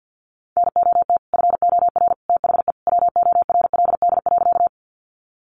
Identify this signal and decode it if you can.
Morse code, heard as a keyed tone, about 50 wpm, 700 Hz. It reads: NOT FOR THE WORLD.
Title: Sentences from Top 600 Words 50wpm